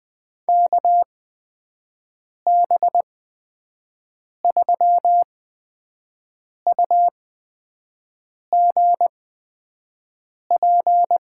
Morse code, heard as a keyed tone, about 20 wpm, spaced as for 5 wpm, 700 Hz. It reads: KB3UGP
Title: USA Call Signs - 6 Characters - 8x Character Spacing - 20wpm